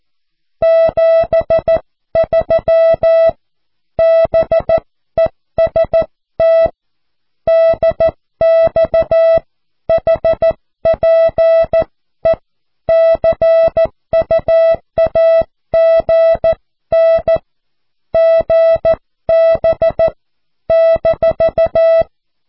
Morse code, heard as a keyed tone, about 14 wpm, 660 Hz. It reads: 73 BEST DX HPE CUAGN GB -
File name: cw / QSO